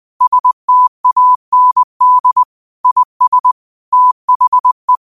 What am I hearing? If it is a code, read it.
Morse code, heard as a keyed tone, about 20 wpm, 1000 Hz. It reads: STAND IS THE
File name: qotdfiles / 20wpm